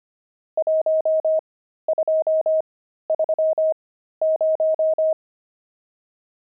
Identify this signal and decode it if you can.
Morse code, heard as a keyed tone, about 25 wpm, spaced as for 13 wpm, 630 Hz. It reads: 1230
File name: pangram-fs-13-cs-25-nums19